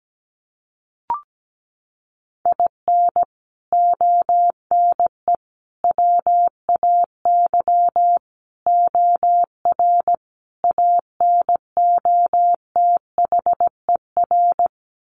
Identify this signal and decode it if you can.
Morse code, heard as a keyed tone, about 17 wpm, 700 Hz. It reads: IN ONE WAY OR ANOTHER
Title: Sets of 5 Words Encore 17wpm